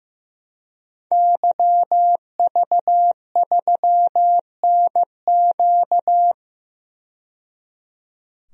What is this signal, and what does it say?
Morse code, heard as a keyed tone, about 15 wpm, 700 Hz. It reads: YV3NQ
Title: DXCC-100 Call Signs - 5 Characters - 15wpm